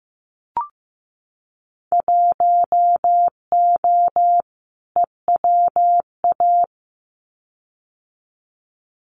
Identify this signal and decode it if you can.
Morse code, heard as a keyed tone, about 15 wpm, 700 Hz. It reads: 1O EWA
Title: Winter Field Day Exchanges Encore 15wpm